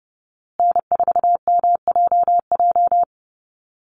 Morse code, heard as a keyed tone, about 30 wpm, 700 Hz. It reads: N4MJJ
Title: USA Call Signs Repeated 3x 30wpm